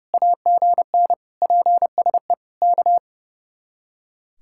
Morse code, heard as a keyed tone, about 38 wpm, 700 Hz. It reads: AGN PSE K